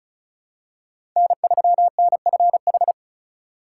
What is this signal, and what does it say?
Morse code, heard as a keyed tone, about 36 wpm, 700 Hz. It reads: N3NFH
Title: USA Call Signs - 4x Character Spacing - 35wpm